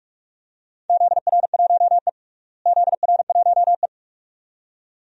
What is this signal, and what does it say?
Morse code, heard as a keyed tone, about 45 wpm, 700 Hz. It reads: ZR1E ZR1E
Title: DXCC-100 Call Signs - 4 Characters - Repeated 2x - 45wpm